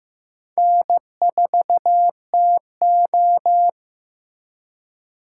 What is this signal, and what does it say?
Morse code, heard as a keyed tone, about 15 wpm, 700 Hz. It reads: N4TO
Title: USA Call Signs - 4 Characters - 8x Character Spacing - 15wpm